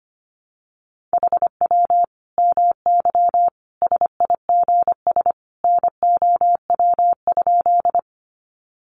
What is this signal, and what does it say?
Morse code, heard as a keyed tone, about 25 wpm, 700 Hz. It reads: HW MY SIGS NOW?
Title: Common QSO Phrases 25wpm